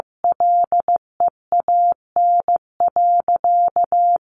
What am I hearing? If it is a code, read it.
Morse code, heard as a keyed tone, about 15 wpm, 700 Hz. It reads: LEAN.
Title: Sentences from Top 1100 Words 15wpm